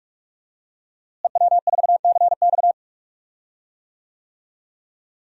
Morse code, heard as a keyed tone, about 41 wpm, 700 Hz. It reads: EW4CX
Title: DXCC-100 Call Signs - 5 Characters - Repeated 2x - 45wpm